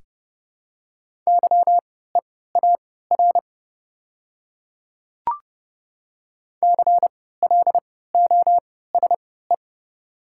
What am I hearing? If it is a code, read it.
Morse code, heard as a keyed tone, about 30 wpm, spaced as for 17 wpm, 700 Hz. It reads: YEAR CLOSE
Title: 3x Character Spacing - Top 500 Words - 30wpm